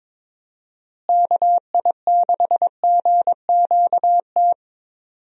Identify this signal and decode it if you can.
Morse code, heard as a keyed tone, about 22 wpm, 700 Hz. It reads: KI6GQT
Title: USA Call Signs - 6 Characters - Repeated 2x - 22wpm